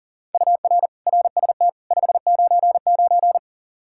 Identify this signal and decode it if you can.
Morse code, heard as a keyed tone, about 40 wpm, 700 Hz. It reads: UR RST 599
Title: K1USN SST QSO Practice 40wpm